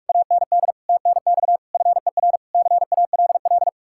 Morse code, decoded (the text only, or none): AND TNX FER CALL